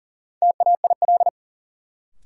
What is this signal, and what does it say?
Morse code, heard as a keyed tone, about 40 wpm, 700 Hz. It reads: TAIL
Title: Sentences from Top 1000 Words - Review Part 3 - New Word Drill - 40wpm